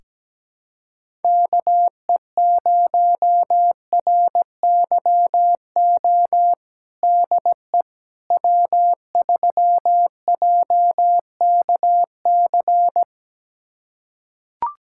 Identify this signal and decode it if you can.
Morse code, heard as a keyed tone, about 17 wpm, 700 Hz. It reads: KE0RYO DE W3JKC
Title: USA Callsign DE Callsign 17wpm